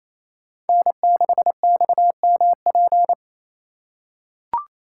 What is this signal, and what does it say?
Morse code, heard as a keyed tone, about 28 wpm, 690 Hz. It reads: N6XMP
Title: USA Call Signs - 5 Characters - 4x Character Spacing - 28wpm